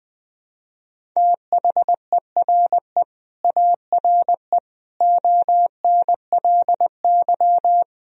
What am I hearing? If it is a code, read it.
Morse code, heard as a keyed tone, about 21 wpm, 700 Hz. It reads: THERE ARE ONLY
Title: Sentences from Top 1000 Words - Review Part 1 - 20wpm